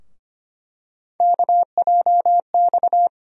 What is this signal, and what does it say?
Morse code, heard as a keyed tone, about 25 wpm, 700 Hz. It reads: KJX
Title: USA Call Sign Postfixes - 3 Letters 25wpm